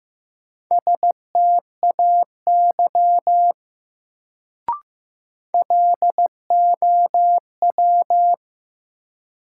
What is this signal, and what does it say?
Morse code, heard as a keyed tone, about 15 wpm, 700 Hz. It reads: STAY LOW
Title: Sentences from Top 500 Words - New Word Drill - 15wpm